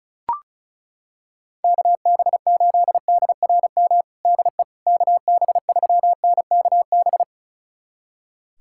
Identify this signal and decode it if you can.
Morse code, heard as a keyed tone, about 34 wpm, 690 Hz. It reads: KB8DRM DE KB3NKB
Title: USA Callsign DE Callsign - 6 Characters - 35wpm